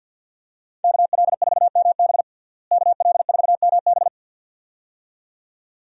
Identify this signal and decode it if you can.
Morse code, heard as a keyed tone, about 52 wpm, 700 Hz. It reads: KL4MB KL4MB
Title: USA Call Signs - 5 Characters - Repeated 2x - 50wpm